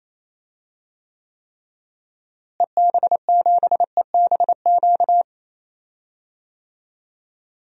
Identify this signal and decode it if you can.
Morse code, heard as a keyed tone, about 28 wpm, 700 Hz. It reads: EB7EBQ
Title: DXCC-100 Call Signs - 6 Characters - Repeated 3x - 28wpm